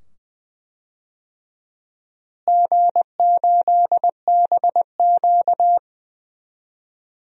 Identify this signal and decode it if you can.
Morse code, heard as a keyed tone, about 20 wpm, 700 Hz. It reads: G8BQ
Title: DXCC-100 Call Signs - 4 Characters - Repeated 2x - 20wpm